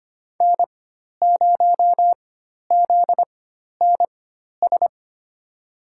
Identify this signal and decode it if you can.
Morse code, heard as a keyed tone, about 25 wpm, spaced as for 12 wpm, 700 Hz. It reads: N0ZNS
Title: USA Call Signs - 5 Characters - 4x Character Spacing - 25wpm